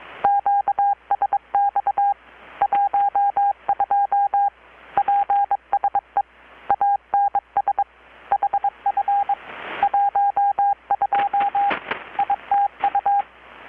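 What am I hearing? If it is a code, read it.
Morse code, heard as a keyed tone, about 22 wpm, 800 Hz, with radio noise. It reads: QSX 12 PSE ANS HF 12 ITU
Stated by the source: Morse from KPH's Night of Nights XX maritime radiotelegraphy event, received in Chiapas, Mexico about 3,800 km away on a Kenwood R-600 with a grounded 10 m long-wire antenna.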